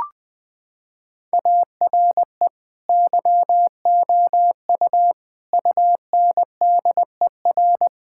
Morse code, heard as a keyed tone, about 20 wpm, 700 Hz. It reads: ARE YOU UNDER
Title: Sentences from Top 1600 Words 20wpm